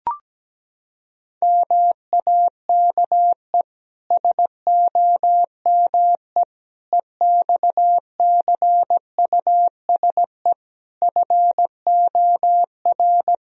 Morse code, heard as a keyed tone, about 17 wpm, 700 Hz. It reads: MAKE SOME EXCUSE FOR
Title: Sentences from Top 1000 Words - Review Part 3 - 17wpm